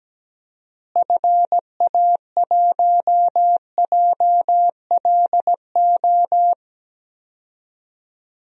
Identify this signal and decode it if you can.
Morse code, heard as a keyed tone, about 17 wpm, 700 Hz. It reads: FA1JLO
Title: DXCC-100 Call Signs - 6 Characters - Repeated 2x - 17wpm